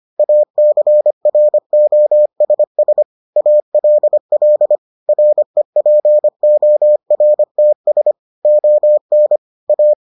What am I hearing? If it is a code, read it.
Morse code, heard as a keyed tone, about 25 wpm, 600 Hz. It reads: ACROSS ALL REPORTS ON A